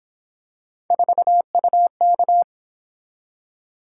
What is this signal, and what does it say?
Morse code, heard as a keyed tone, about 28 wpm, 700 Hz. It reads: 4UK